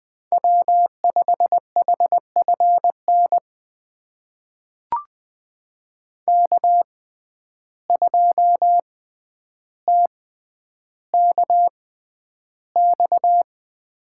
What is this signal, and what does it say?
Morse code, heard as a keyed tone, about 20 wpm, spaced as for 10 wpm, 700 Hz. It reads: W5HFN K2TKX
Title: USA Call Signs - 5 Characters - 6x Character Spacing - 20wpm